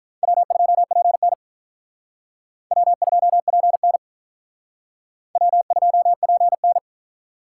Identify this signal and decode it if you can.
Morse code, heard as a keyed tone, about 43 wpm, 700 Hz. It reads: W2PN W2PN W2PN
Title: Speed Racing - USA 1x2 Call Signs 35wpm